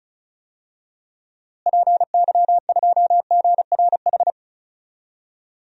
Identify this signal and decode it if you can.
Morse code, heard as a keyed tone, about 36 wpm, 700 Hz. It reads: PY2GRH
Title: DXCC-100 Call Signs - 6 Characters - Repeated 2x - 35wpm